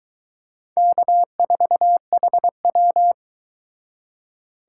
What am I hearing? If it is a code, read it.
Morse code, heard as a keyed tone, about 22 wpm, 700 Hz. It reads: K4HW